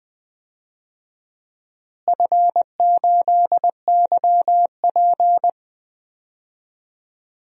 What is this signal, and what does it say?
Morse code, heard as a keyed tone, about 20 wpm, 700 Hz. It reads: F8YP
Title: DXCC-100 Call Signs - 4 Characters - Repeated 2x - 20wpm